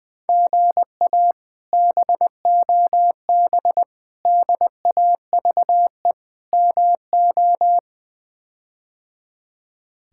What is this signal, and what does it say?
Morse code, heard as a keyed tone, about 20 wpm, 700 Hz. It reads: GA BOB DAVE MO
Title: K1USN SST QSO Practice - 20wpm